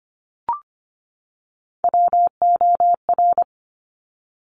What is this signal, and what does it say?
Morse code, heard as a keyed tone, about 25 wpm, 700 Hz. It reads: WOR
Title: Top Three Letter Combinations in Words 25wpm